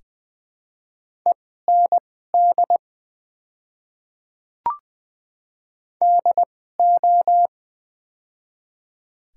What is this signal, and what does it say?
Morse code, heard as a keyed tone, about 20 wpm, spaced as for 14 wpm, 700 Hz. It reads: END DO
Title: 2x Character Spacing - Top 200 Words - 20wpm